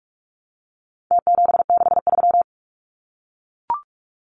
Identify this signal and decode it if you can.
Morse code, heard as a keyed tone, about 44 wpm, 700 Hz. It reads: T763
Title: Numbers - Contest Style - 4-Digit 45wpm